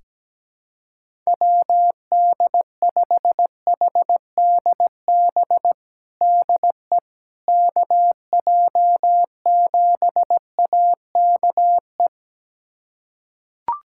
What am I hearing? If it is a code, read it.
Morse code, heard as a keyed tone, about 17 wpm, 700 Hz. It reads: WD5HDB DE KJ7AKE
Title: USA Callsign DE Callsign - 6 Characters - 17wpm